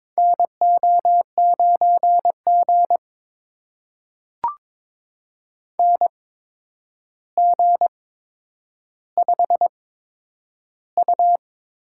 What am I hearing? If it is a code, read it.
Morse code, heard as a keyed tone, about 22 wpm, spaced as for 9 wpm, 700 Hz. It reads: NO9G NG5U
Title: USA Call Signs - 4 Characters - 8x Character Spacing - 22wpm